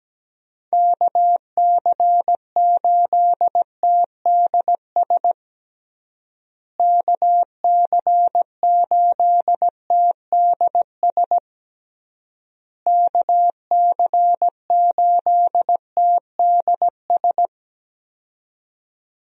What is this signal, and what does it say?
Morse code, heard as a keyed tone, about 17 wpm, 700 Hz. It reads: KC8TDS KC8TDS KC8TDS
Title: USA Call Signs - 6 Characters - Repeated 3x - 17wpm